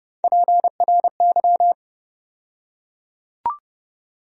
Morse code, heard as a keyed tone, about 31 wpm, 700 Hz. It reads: PRY
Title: USA Call Sign Postfixes - 3 Letters 30wpm